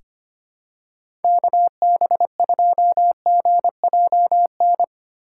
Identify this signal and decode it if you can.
Morse code, heard as a keyed tone, about 25 wpm, 700 Hz. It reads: KB2GJN
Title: USA Call Signs - 6 Characters - 4x Character Spacing - 25wpm